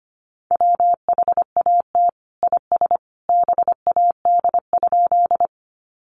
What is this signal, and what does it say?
Morse code, heard as a keyed tone, about 25 wpm, 700 Hz. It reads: WHAT IS BAD?
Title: Sentences from Top 700 Words 25wpm